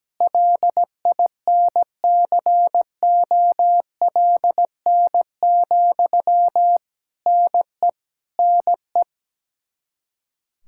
Morse code, heard as a keyed tone, about 17 wpm, 700 Hz. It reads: LINCOLN, NE NE
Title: Practice QSO - Exchange 1 - Beginning - 17wpm